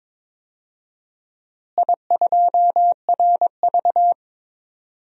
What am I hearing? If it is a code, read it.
Morse code, heard as a keyed tone, about 22 wpm, 700 Hz. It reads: I2RV
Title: DXCC-100 Call Signs - 4 Characters - Repeated 2x - 22wpm